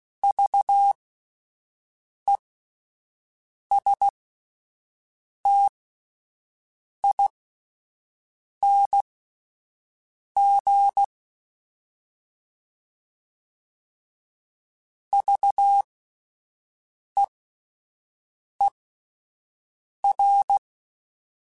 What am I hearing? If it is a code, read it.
Morse code, heard as a keyed tone, about 16 wpm, spaced as for 5 wpm, 800 Hz. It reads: VESTING VEER